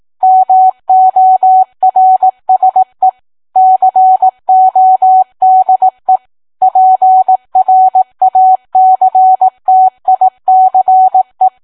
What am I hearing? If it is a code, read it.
Morse code, heard as a keyed tone, about 18 wpm, 750 Hz. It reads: MORSE CODE PRACTICE